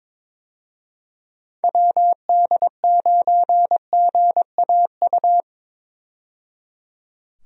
Morse code, heard as a keyed tone, about 22 wpm, 700 Hz. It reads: WD9GAU
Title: USA Call Signs - 6 Characters - 22wpm